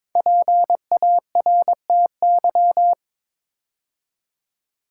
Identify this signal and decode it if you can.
Morse code, heard as a keyed tone, about 22 wpm, 700 Hz. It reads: PARTY
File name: 5 Letter Words - First 3 Letters Given - 22wpm